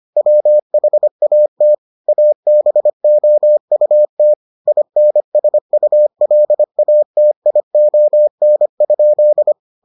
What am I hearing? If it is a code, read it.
Morse code, heard as a keyed tone, about 25 wpm, 600 Hz. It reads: WHAT ABOUT INSULATION?